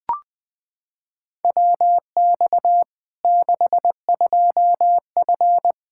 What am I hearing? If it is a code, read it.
Morse code, heard as a keyed tone, about 21 wpm, 700 Hz. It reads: WX 62F
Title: Practice QSO - Exchange 2 - Info - 20wpm